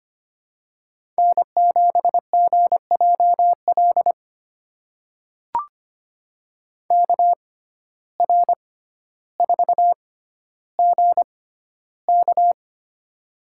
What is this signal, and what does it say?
Morse code, heard as a keyed tone, about 25 wpm, spaced as for 13 wpm, 700 Hz. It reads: N7GJL KR4GK
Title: USA Call Signs - 5 Characters - 6x Character Spacing - 25wpm